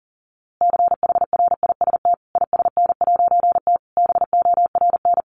Morse code, heard as a keyed tone, about 37 wpm, 700 Hz. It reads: CHRIST ISN'T BORN
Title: A Sound Of Thunder 40wpm